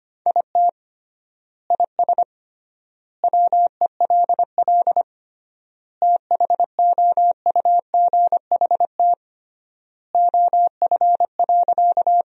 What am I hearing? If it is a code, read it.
Morse code, heard as a keyed tone, about 25 wpm, 700 Hz. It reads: IT IS WELL THOUGHT OF.
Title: 3x Word-Spacing - Sentences from Top 300 Words - 25wpm